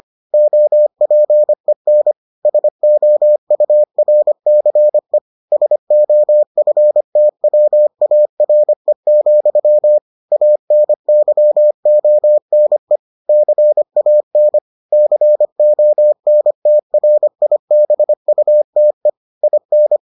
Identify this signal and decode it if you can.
Morse code, heard as a keyed tone, about 25 wpm, 600 Hz. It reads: OPEN SOURCE SOFTWARE, ANYONE CAN CONTRIBUTE IN